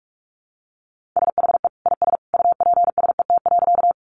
Morse code, heard as a keyed tone, about 44 wpm, 700 Hz. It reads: SHE IS UPSET.
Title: Sentences from Top 800 Words 45wpm